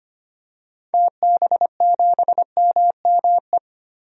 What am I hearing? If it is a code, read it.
Morse code, heard as a keyed tone, about 25 wpm, 690 Hz. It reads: TB7MME
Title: DXCC-100 Call Signs - 6 Characters - Repeated 3x - 25wpm